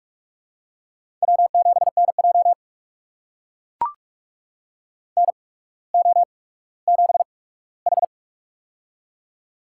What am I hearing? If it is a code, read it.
Morse code, heard as a keyed tone, about 44 wpm, spaced as for 19 wpm, 700 Hz. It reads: W7NJ NO7H
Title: USA Call Signs - 4 Characters - 8x Character Spacing - 45wpm